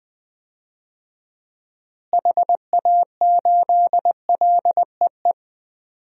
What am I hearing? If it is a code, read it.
Morse code, heard as a keyed tone, about 20 wpm, 700 Hz. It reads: HA8LEE
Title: DXCC-100 Call Signs - 6 Characters - Repeated 2x - 20wpm